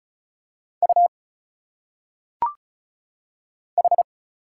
Morse code, heard as a keyed tone, about 36 wpm, 700 Hz. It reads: U H
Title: Single Letters - Phonetic Pronunciation 35wpm